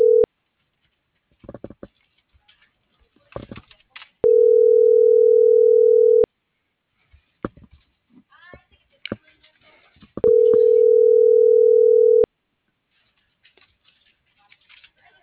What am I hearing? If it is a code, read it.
Ringback tone (North America), 440 + 480 Hz.